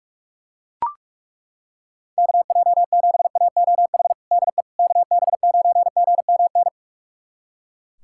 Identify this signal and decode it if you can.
Morse code, heard as a keyed tone, about 41 wpm, 700 Hz. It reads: KJ7AOH DE KB9GMN
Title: USA Callsign DE Callsign - 6 Characters - 45wpm